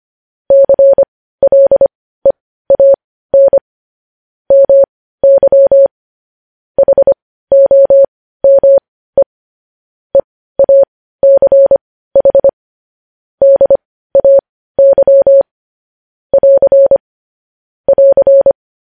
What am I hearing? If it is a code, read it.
Morse code, heard as a keyed tone, about 25 wpm, spaced as for 15 wpm, 550 Hz. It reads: CLEAN MY HOME EACH DAY <AR> <AR>
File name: podcast / News headlines-1x-25x15